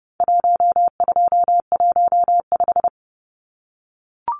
Morse code, heard as a keyed tone, about 30 wpm, 700 Hz. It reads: 1215
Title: Numbers - 4-Digit 30wpm